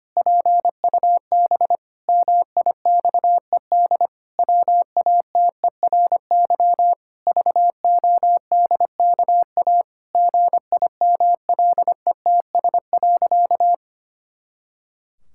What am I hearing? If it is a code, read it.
Morse code, heard as a keyed tone, about 25 wpm, 700 Hz. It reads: PUB MIXED WATERY VODKA GIMLETS.
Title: Pangram Challenge v2 - 25wpm